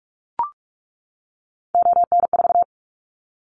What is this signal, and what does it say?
Morse code, heard as a keyed tone, about 44 wpm, 700 Hz. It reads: ON4
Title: Callsign Prefixes for Top 50 DXCC - 45wpm